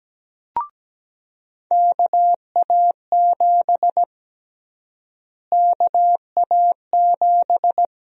Morse code, heard as a keyed tone, about 17 wpm, 700 Hz. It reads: KA7 KA7
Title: USA Call Sign Prefixes - Repeated 2x - 17wpm